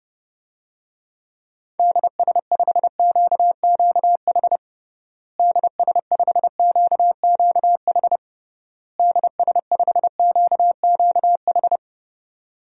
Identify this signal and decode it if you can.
Morse code, heard as a keyed tone, about 30 wpm, 700 Hz. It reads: DS5QQH DS5QQH DS5QQH
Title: DXCC-100 Call Signs - 6 Characters - Repeated 3x - 30wpm